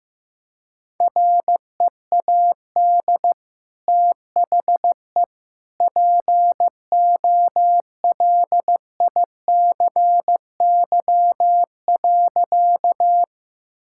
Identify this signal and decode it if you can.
Morse code, heard as a keyed tone, about 15 wpm, 700 Hz. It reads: READ THE POLICY.
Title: Sentences from Top 1000 Words - Review Part 5 - 15wpm